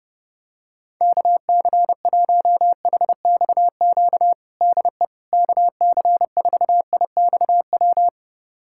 Morse code, heard as a keyed tone, about 30 wpm, 700 Hz. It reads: KC1HXQ DE KC4IXW
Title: USA Callsign DE Callsign - 6 Characters - 30wpm